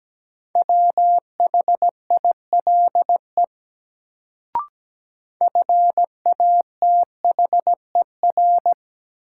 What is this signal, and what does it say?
Morse code, heard as a keyed tone, about 17 wpm, 700 Hz. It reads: WHILE FATHER
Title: Sentences from Top 300 Words - New Word Drill - 17wpm